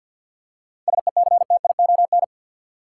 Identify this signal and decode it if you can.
Morse code, heard as a keyed tone, about 52 wpm, 700 Hz. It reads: SECTION
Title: Seven Letter Words - 50wpm